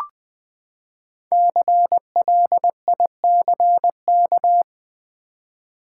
Morse code, heard as a keyed tone, about 20 wpm, 700 Hz. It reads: CLICK
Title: Sentences from Top 1100 Words - New Word Drill v2 - 20wpm